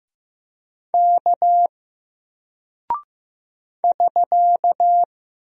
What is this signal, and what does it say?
Morse code, heard as a keyed tone, about 15 wpm, 700 Hz. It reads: K <SK>
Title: Single Character (Letter:Number:Punctuation:Prosign) Alt 15wpm